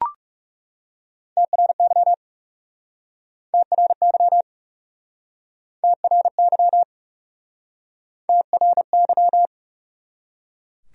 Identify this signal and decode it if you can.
Morse code, heard as a keyed tone, about 40 wpm, 700 Hz. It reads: TRY TRY TRY TRY